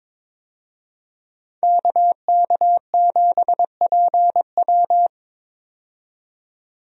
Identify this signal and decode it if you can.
Morse code, heard as a keyed tone, about 22 wpm, 700 Hz. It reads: KK7PW
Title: USA Call Signs - 5 Characters - 22wpm